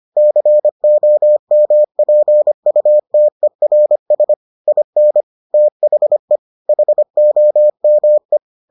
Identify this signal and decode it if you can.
Morse code, heard as a keyed tone, about 25 wpm, 600 Hz. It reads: COMPUTERS IN THE HOME